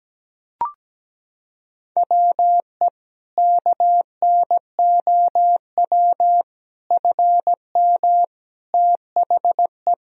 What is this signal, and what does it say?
Morse code, heard as a keyed tone, about 17 wpm, 700 Hz. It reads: WE KNOW FM THE
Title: Sentences from Top 500 Words and Abbreviations 17wpm